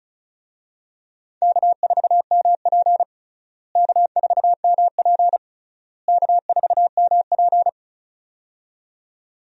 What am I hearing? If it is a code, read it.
Morse code, heard as a keyed tone, about 34 wpm, 700 Hz. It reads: K4MP K4MP K4MP
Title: USA Call Signs - 4 Characters - Repeated 3x - 35wpm